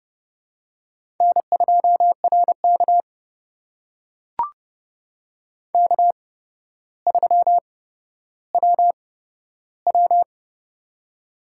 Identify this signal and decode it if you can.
Morse code, heard as a keyed tone, about 30 wpm, spaced as for 13 wpm, 700 Hz. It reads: N2RK K3WW
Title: USA Call Signs - 4 Characters - 8x Character Spacing - 30wpm